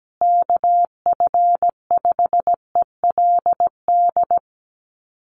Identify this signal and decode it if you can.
Morse code, heard as a keyed tone, about 17 wpm, 700 Hz. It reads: KF5ELD
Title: USA Call Signs Repeated 2x 17wpm